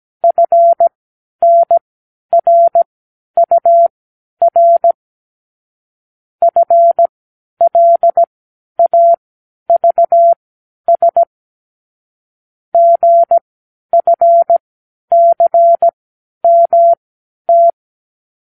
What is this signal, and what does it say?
Morse code, heard as a keyed tone, about 17 wpm, spaced as for 10 wpm, 680 Hz. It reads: FNRUR FLAVS GFCMT
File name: k7qo / Lesson 36 (A-X)